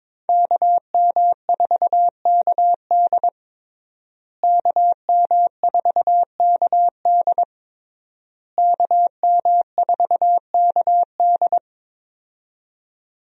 Morse code, heard as a keyed tone, about 22 wpm, 700 Hz. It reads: KM4KD KM4KD KM4KD
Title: USA Call Signs - 5 Characters - Repeated 3x - 22wpm